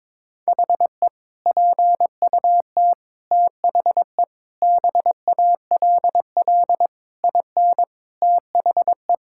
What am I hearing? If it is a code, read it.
Morse code, heard as a keyed tone, about 22 wpm, 700 Hz. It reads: HE PUT THE BALL IN THE
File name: practice / 22wpm / Sentences with 8-Word Phrases - 22wpm